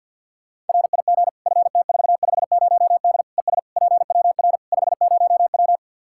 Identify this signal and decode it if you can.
Morse code, heard as a keyed tone, about 52 wpm, 700 Hz. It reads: RIG FT450D ES PWR 50W
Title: K1USN SST QSO Practice 50wpm